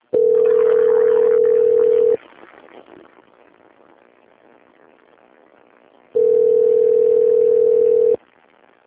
Ringback tone (North America), 440 + 480 Hz.
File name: out- - - - - .37528